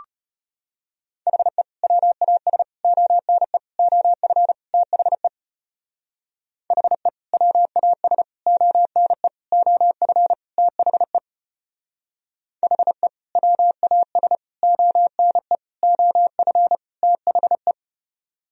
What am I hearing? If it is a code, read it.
Morse code, heard as a keyed tone, about 33 wpm, 700 Hz. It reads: HE WAS ONE OF THE HE WAS ONE OF THE HE WAS ONE OF THE